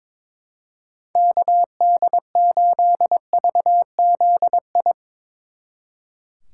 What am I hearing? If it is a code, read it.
Morse code, heard as a keyed tone, about 24 wpm, 700 Hz. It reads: KD8VZI
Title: Speed Racing - USA 2x3 Call Signs 22wpm